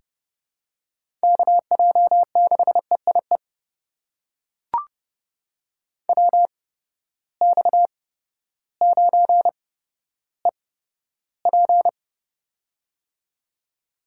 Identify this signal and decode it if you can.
Morse code, heard as a keyed tone, about 30 wpm, spaced as for 14 wpm, 700 Hz. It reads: KJ6EIE WX9EP
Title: USA Call Signs - 8x Character Spacing - 30wpm